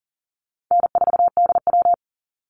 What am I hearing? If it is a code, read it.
Morse code, heard as a keyed tone, about 40 wpm, 700 Hz. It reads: N4DW
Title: Easy Call Signs Repeated 3x - 40wpm